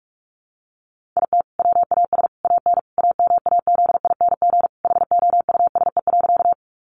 Morse code, heard as a keyed tone, about 44 wpm, 700 Hz. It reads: IT WAS AN AMAZING HOUSE.
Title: Sentences from Top 1500 Words 45wpm